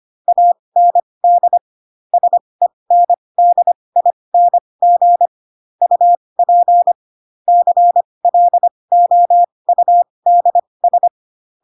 Morse code, heard as a keyed tone, about 25 wpm, spaced as for 20 wpm, 700 Hz. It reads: AND SENDING UP CLOUDS